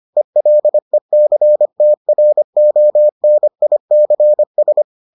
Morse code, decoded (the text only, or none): ELECTRONICS